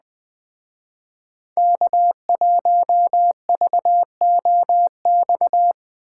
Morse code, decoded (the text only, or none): K1VOX